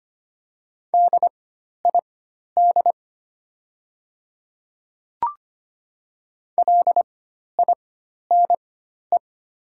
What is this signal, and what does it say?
Morse code, heard as a keyed tone, about 25 wpm, spaced as for 12 wpm, 700 Hz. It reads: DID LINE
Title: 4x Character Spacing - Top 300 Words - 25wpm